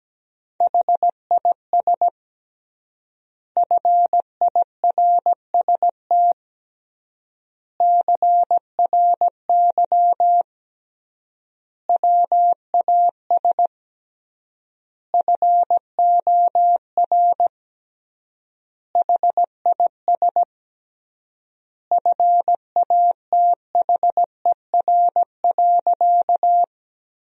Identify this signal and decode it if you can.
Morse code, heard as a keyed tone, about 17 wpm, 700 Hz. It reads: HIS FIRST CRY WAS FOR HIS FATHER.
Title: 3x Word-Spacing - Sentences from Top 500 Words - 17wpm